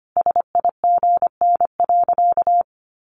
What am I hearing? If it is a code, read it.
Morse code, heard as a keyed tone, about 25 wpm, 690 Hz. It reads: SIGN.
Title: Sentences from Top 1100 Words 25wpm